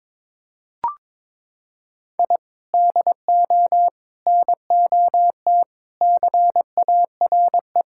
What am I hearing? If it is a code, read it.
Morse code, heard as a keyed tone, about 22 wpm, 700 Hz. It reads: I DO NOT CARE
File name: practice / 22wpm / Sentences with 4-Word Phrases - 22wpm